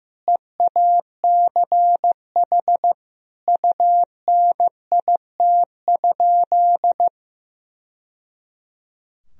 Morse code, heard as a keyed tone, about 14 wpm, 700 Hz. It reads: EACH UNIT?
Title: Sentences from Top 1000 Words - Review Part 2 - 15wpm